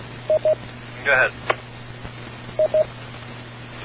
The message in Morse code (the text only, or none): none